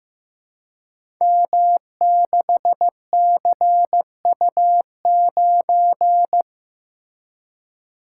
Morse code, heard as a keyed tone, about 15 wpm, 700 Hz. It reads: M6CU9
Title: Code Groups - 5 Letter:Numbers - 15wpm